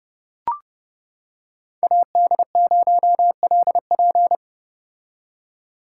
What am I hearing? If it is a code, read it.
Morse code, heard as a keyed tone, about 30 wpm, 700 Hz. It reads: AD0LP